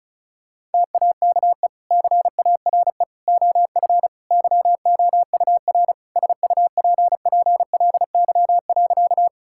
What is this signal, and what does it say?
Morse code, heard as a keyed tone, about 36 wpm, 700 Hz. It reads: TAKE CARE OF YOUR SUPPLY.
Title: Sentences from Top 1000 Words - Review Part 5 - 35wpm